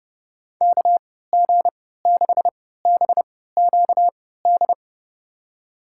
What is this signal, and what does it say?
Morse code, heard as a keyed tone, about 30 wpm, spaced as for 17 wpm, 700 Hz. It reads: KG6BQD
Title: USA Call Signs - 6 Characters - 3x Character Spacing - 30wpm